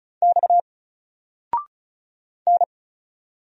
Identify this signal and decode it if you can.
Morse code, heard as a keyed tone, about 36 wpm, 700 Hz. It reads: X N
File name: Single-Letter-Number-35wpm